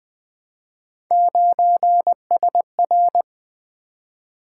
Morse code, heard as a keyed tone, about 20 wpm, 700 Hz. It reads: 9SR
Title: Three Letter-Number Combinations - 20wpm